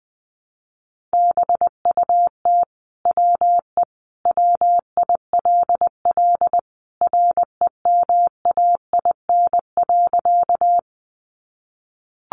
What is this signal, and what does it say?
Morse code, heard as a keyed tone, about 20 wpm, 700 Hz. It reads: BUT WE WILL REMAIN.
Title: Sentences from Top 1200 Words 20wpm